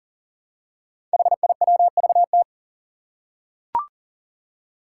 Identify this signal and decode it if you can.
Morse code, heard as a keyed tone, about 40 wpm, 700 Hz. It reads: HIWVT
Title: Five Challenging Letter-Number Combinations 40wpm